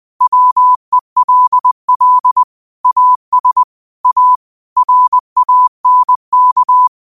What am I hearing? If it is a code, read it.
Morse code, heard as a keyed tone, about 20 wpm, 1000 Hz. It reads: WELL AS A RANK